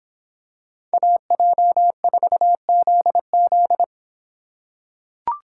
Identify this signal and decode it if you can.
Morse code, heard as a keyed tone, about 26 wpm, 700 Hz. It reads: AJ4ZZ